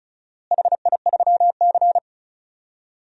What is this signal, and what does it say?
Morse code, heard as a keyed tone, about 36 wpm, 700 Hz. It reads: HI3C
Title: DXCC-100 Call Signs - 4 Characters - Repeated 2x - 35wpm